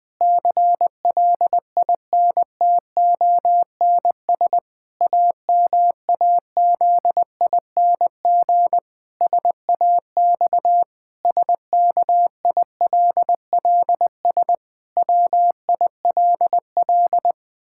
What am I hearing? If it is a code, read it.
Morse code, heard as a keyed tone, about 20 wpm, 700 Hz. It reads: CLINTONS AMAZING SAX SKILLS WILL
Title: Pangram Challenge v2 - 20wpm